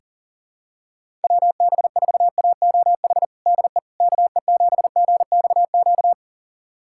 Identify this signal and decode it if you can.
Morse code, heard as a keyed tone, about 40 wpm, 700 Hz. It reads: WB4AOH DE KE7GXQ
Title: USA Callsign DE Callsign 40wpm